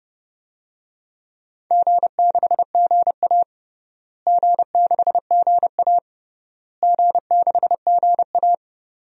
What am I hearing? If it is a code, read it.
Morse code, heard as a keyed tone, about 32 wpm, 700 Hz. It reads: G6GA G6GA G6GA
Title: DXCC-100 Call Signs - 4 Characters - Repeated 3x - 30wpm